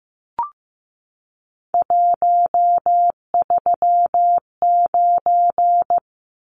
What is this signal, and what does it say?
Morse code, heard as a keyed tone, about 15 wpm, 700 Hz. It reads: 139
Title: Numbers - 3-Digit 15wpm